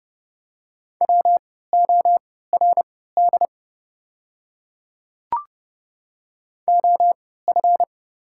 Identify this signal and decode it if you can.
Morse code, heard as a keyed tone, about 30 wpm, spaced as for 17 wpm, 690 Hz. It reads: WORD OF
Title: 3x Character Spacing - Top 100 Words - 30wpm